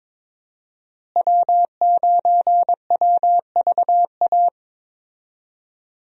Morse code, heard as a keyed tone, about 22 wpm, 700 Hz. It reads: W9WVA
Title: USA Call Signs - 5 Characters - Repeated 2x - 22wpm